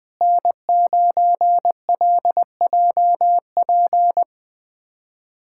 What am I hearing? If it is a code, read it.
Morse code, heard as a keyed tone, about 21 wpm, 700 Hz. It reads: N9LJP
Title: USA Callsign DE Callsign 20wpm